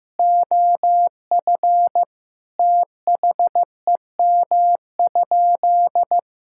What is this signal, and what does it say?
Morse code, heard as a keyed tone, about 15 wpm, 700 Hz. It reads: OF THEM?
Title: Sentences from Top 1000 Words - Review Part 6 - 15wpm